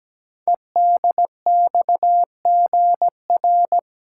Morse code, heard as a keyed tone, about 17 wpm, 700 Hz. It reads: EDXGR
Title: Code Groups - 5 Letters - 17wpm